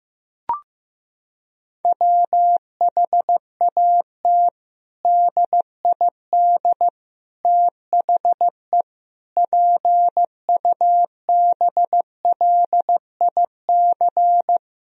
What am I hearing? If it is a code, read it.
Morse code, heard as a keyed tone, about 15 wpm, 700 Hz. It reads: WHAT DID THE PUBLIC
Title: Sentences from Top 1000 Words - Review Part 3 - 15wpm